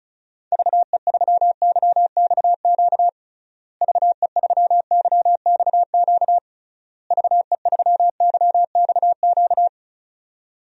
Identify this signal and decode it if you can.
Morse code, heard as a keyed tone, about 34 wpm, 700 Hz. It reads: VE3YXQ VE3YXQ VE3YXQ
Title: DXCC-100 Call Signs - 6 Characters - Repeated 3x - 35wpm